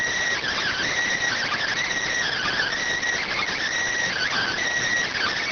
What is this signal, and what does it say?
Emergency Alert System (SAME) tones, heard as harsh buzzing bursts. It is heard from 0.4 to 4.6 s.